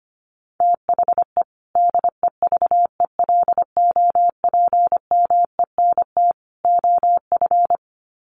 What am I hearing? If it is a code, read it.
Morse code, heard as a keyed tone, about 25 wpm, 700 Hz. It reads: THE DEVELOPMENT OF
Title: Sets of 3 Words 25wpm